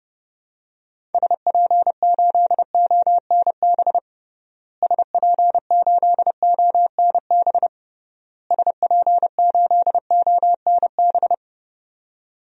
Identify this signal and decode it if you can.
Morse code, heard as a keyed tone, about 32 wpm, 700 Hz. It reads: SP8ONB SP8ONB SP8ONB
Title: DXCC-100 Call Signs - 6 Characters - Repeated 3x - 30wpm